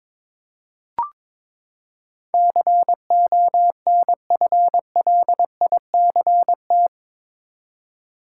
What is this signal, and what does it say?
Morse code, heard as a keyed tone, about 22 wpm, 700 Hz. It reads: CONFLICT
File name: practice / 22wpm / Eight Letter Words - 22wpm